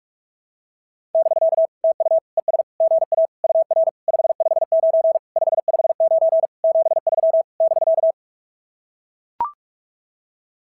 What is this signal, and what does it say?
Morse code, heard as a keyed tone, about 44 wpm, 660 Hz. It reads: <BK> TU ES GA UR 559 559 73 <BK>
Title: SOTA QSO Practice 45wpm